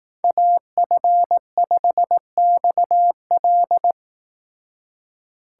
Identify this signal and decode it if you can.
Morse code, heard as a keyed tone, about 18 wpm, 700 Hz. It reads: AF5XL
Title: Speed Racing - USA 2x2 Call Signs 15wpm